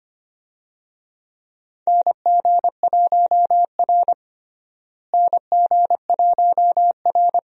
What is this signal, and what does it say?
Morse code, heard as a keyed tone, about 25 wpm, 700 Hz. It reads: NG1R NG1R
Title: USA Call Signs - 4 Characters - Repeated 2x - 25wpm